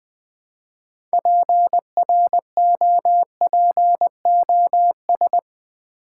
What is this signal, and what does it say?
Morse code, heard as a keyed tone, about 20 wpm, 700 Hz. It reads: PROPOS
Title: Build A Word - 2 Letters At A Time - 8-Letter Words - 20wpm